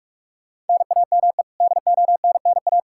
Morse code, heard as a keyed tone, about 46 wpm, 700 Hz. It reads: NAME DONNA
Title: Practice QSO - Exchange 1 - Beginning - 45wpm